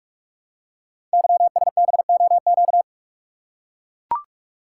Morse code, heard as a keyed tone, about 44 wpm, 700 Hz. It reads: YSBOQ
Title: Code Groups - 5 Letters - 45wpm